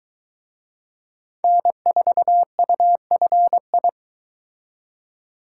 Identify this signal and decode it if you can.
Morse code, heard as a keyed tone, about 23 wpm, 700 Hz. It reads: N4UFI